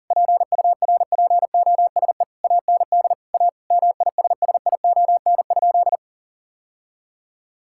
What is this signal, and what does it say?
Morse code, heard as a keyed tone, about 40 wpm, 700 Hz. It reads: PURPOSE AND A MISSION?
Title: Sentences from Top 1600 Words 40wpm